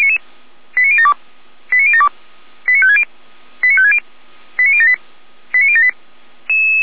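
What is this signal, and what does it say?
Five-tone selective call (ZVEI), heard as quick runs of five tones, tones of about 70 ms. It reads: 89072 89072 89560 89560 89078 89078
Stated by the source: Five-tone ZVEI selective call received by SDR.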